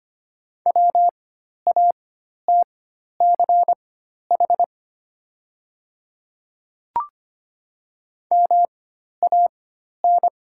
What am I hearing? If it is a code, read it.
Morse code, heard as a keyed tone, about 25 wpm, spaced as for 12 wpm, 700 Hz. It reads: WATCH MAN
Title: 4x Character Spacing - Top 300 Words - 25wpm